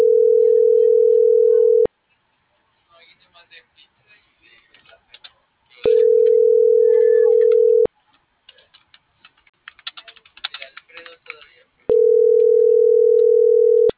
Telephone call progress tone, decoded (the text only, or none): ringback tone